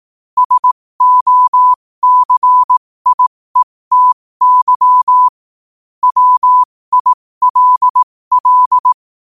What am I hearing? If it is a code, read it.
Morse code, heard as a keyed tone, about 18 wpm, spaced as for 15 wpm, 1000 Hz. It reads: SOCIETY WILL
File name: qotdfiles / 15wpm